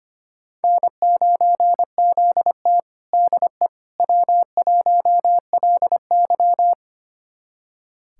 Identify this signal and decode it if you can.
Morse code, heard as a keyed tone, about 25 wpm, 700 Hz. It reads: N9ZT DE W1LY
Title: USA Callsign DE Callsign - 4 Characters - 25wpm